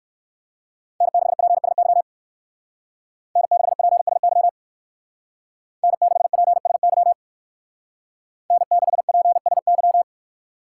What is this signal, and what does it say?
Morse code, heard as a keyed tone, about 67 wpm, 700 Hz. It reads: N6PSY N6PSY N6PSY N6PSY